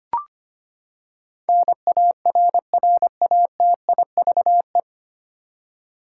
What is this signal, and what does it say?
Morse code, heard as a keyed tone, about 25 wpm, 700 Hz. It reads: NARRATIVE
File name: Nine Letter Words - 25wpm